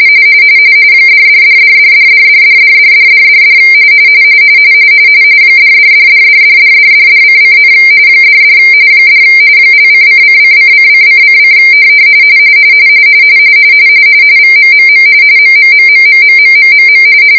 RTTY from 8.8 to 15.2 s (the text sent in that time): $ ):::::::::: VMMM